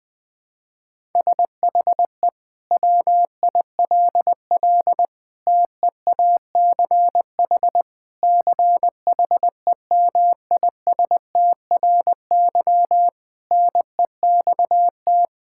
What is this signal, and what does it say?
Morse code, heard as a keyed tone, about 20 wpm, 700 Hz. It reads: SHE WILL TEACH CHEMISTRY NEXT
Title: Sentences from Top 1000 Words - Review Part 6 - 20wpm